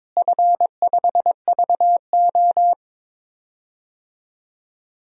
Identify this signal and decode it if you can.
Morse code, heard as a keyed tone, about 22 wpm, 700 Hz. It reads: F5VO
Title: DXCC-100 Call Signs - 4 Characters - Repeated 3x - 22wpm